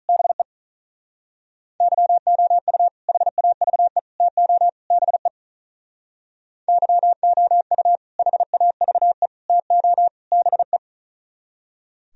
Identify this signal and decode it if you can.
Morse code, heard as a keyed tone, about 44 wpm, 700 Hz. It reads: BE YOU HAVE TO BE YOU HAVE TO BE
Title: Speed-Racing - Sets of 4 Words 35wpm